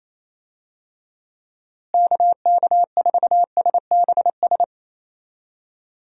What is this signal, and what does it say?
Morse code, heard as a keyed tone, about 28 wpm, 700 Hz. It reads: KK4SBS
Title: USA Call Signs - 6 Characters - Repeated 3x - 28wpm